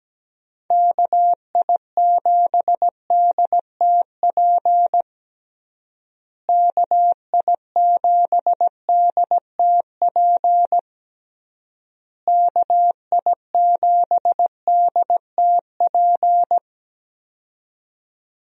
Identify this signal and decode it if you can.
Morse code, heard as a keyed tone, about 17 wpm, 700 Hz. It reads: KI7DTP KI7DTP KI7DTP
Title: USA Call Signs - 6 Characters - Repeated 3x - 17wpm